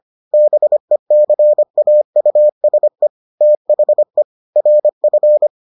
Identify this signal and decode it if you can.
Morse code, heard as a keyed tone, about 25 wpm, 600 Hz. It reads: BECAUSE THE RF